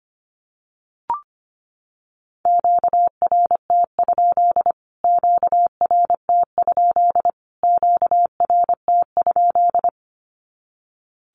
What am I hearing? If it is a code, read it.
Morse code, heard as a keyed tone, about 25 wpm, 700 Hz. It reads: QRT? QRT? QRT?
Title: Q-Codes Repeated 3x 25wpm